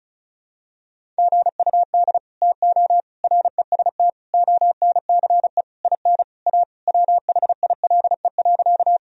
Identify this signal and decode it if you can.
Morse code, heard as a keyed tone, about 36 wpm, 700 Hz. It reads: GUD TO REST ONCE IN A WHILE.
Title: Sentences from Top 500 Words and Abbreviations 35wpm